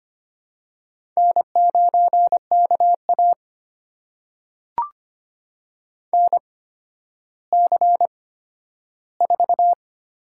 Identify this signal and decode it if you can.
Morse code, heard as a keyed tone, about 25 wpm, spaced as for 12 wpm, 700 Hz. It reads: N9KA NC4
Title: USA Call Signs - 4 Characters - 8x Character Spacing - 25wpm